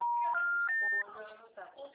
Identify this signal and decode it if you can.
Special information tone, with noise.